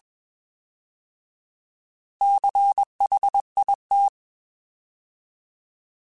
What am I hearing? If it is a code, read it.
Morse code, heard as a keyed tone, about 21 wpm, 800 Hz. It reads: CHIT